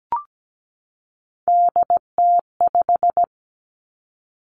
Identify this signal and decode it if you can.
Morse code, heard as a keyed tone, about 17 wpm, 700 Hz. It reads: DT5
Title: Three Challenging Letter-Number Combinations 17wpm